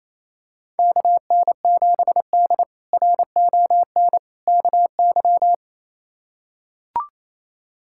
Morse code, heard as a keyed tone, about 28 wpm, 700 Hz. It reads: KN7D RON KY
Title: K1USN SST QSO Practice - 28wpm